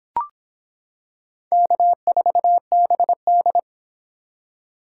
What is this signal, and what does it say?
Morse code, heard as a keyed tone, about 25 wpm, 700 Hz. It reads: K4BD